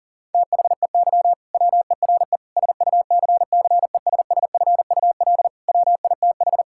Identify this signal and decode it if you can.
Morse code, heard as a keyed tone, about 40 wpm, 700 Hz. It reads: THEY WERE SUCCESSFUL WITH
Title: Sentences from Top 1000 Words - Review Part 6 - 40wpm